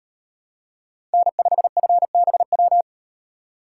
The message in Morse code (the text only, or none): N5FBW